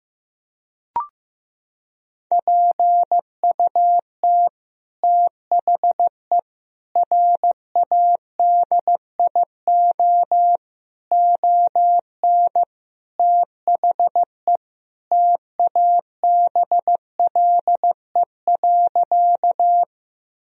Morse code, heard as a keyed tone, about 15 wpm, 700 Hz. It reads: PUT THE RADIO ON THE TABLE.
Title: Sentences from Top 1000 Words - Review Part 3 - 15wpm